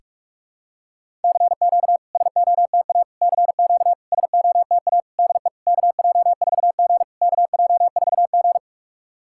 Morse code, heard as a keyed tone, about 44 wpm, 690 Hz. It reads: CQ SOTA CQ SOTA DE KJ4G KJ4G
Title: SOTA QSO Practice 45wpm